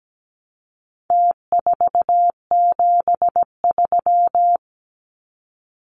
Morse code, heard as a keyed tone, about 17 wpm, 700 Hz. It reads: T473
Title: Numbers - Contest Style - 4-Digit 17wpm